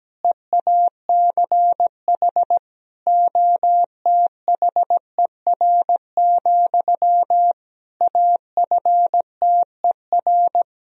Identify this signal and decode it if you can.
Morse code, heard as a keyed tone, about 17 wpm, 700 Hz. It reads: EACH OTHER, AFTER
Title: Sentences from Top 1000 Words - Review Part 1 - 17wpm